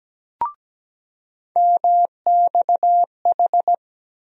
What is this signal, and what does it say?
Morse code, heard as a keyed tone, about 17 wpm, 700 Hz. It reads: MXH
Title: USA Call Sign Postfixes - 3 Letters 17wpm